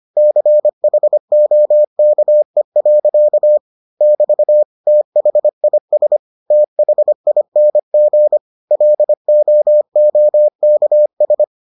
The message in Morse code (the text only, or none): CHOKE. <BT> THIS THING LOOKS